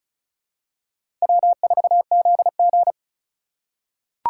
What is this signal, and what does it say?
Morse code, heard as a keyed tone, about 36 wpm, 700 Hz. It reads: W4ZG
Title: USA Call Signs - 3x Character Spacing - 35wpm